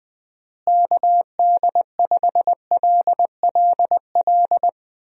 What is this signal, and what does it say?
Morse code, heard as a keyed tone, about 21 wpm, 700 Hz. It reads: KD5LLL
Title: USA Call Signs - 6 Characters - Repeated 3x - 20wpm